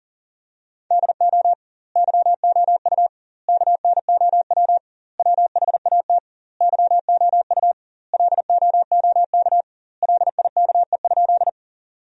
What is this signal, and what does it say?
Morse code, heard as a keyed tone, about 40 wpm, 700 Hz. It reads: DO YOU KNOW WHAT YOU LOOK LIKE?
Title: 2x Word-Spacing - Sentences from Top 100 Words - 40wpm